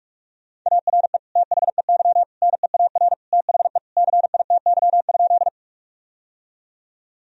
Morse code, heard as a keyed tone, about 42 wpm, 700 Hz. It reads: ARE THEY NEAR THE CITY?
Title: Sentences from Top 1000 Words - Review Part 1 - 45wpm